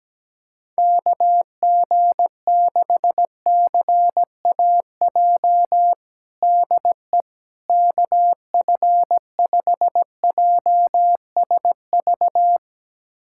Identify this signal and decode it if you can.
Morse code, heard as a keyed tone, about 17 wpm, 700 Hz. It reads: KG6CAJ DE KF5JSV
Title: USA Callsign DE Callsign - 6 Characters - 17wpm